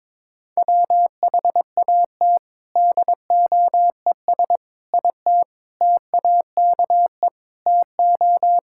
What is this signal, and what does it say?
Morse code, heard as a keyed tone, about 22 wpm, 700 Hz. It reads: WHAT DOES IT TAKE TO
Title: Sentences from Top 1600 Words 22wpm